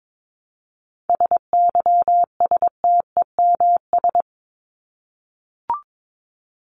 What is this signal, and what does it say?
Morse code, heard as a keyed tone, about 22 wpm, 700 Hz. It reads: SYSTEMS
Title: ES:S Suffix - Top 100 Words with Base Words 22wpm